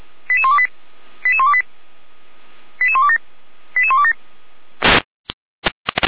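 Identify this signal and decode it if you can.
Five-tone selective call (ZVEI), heard as quick runs of five tones, tones of about 70 ms. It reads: 88128 88128 88127 88127
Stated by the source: Five-tone ZVEI selective call 88103 received on an SDR, from the feuerwehr-orlamuende.org archive.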